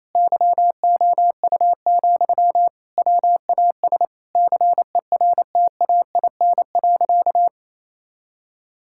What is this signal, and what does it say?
Morse code, heard as a keyed tone, about 28 wpm, 700 Hz. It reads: YOU, WAS CERTAIN.
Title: Sentences from Top 1000 Words - Review Part 6 - 28wpm